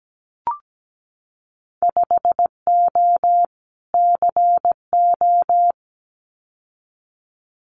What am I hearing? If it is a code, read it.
Morse code, heard as a keyed tone, about 17 wpm, 700 Hz. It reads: 5O CO
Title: Winter Field Day Exchanges Encore 17wpm